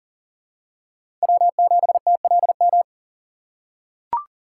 Morse code, heard as a keyed tone, about 40 wpm, 700 Hz. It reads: W7TLM
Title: USA Call Signs - 5 Characters - 3x Character Spacing - 40wpm